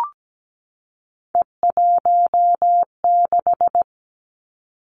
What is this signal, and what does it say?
Morse code, heard as a keyed tone, about 17 wpm, 700 Hz. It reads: E16
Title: Three Challenging Letter-Number Combinations 17wpm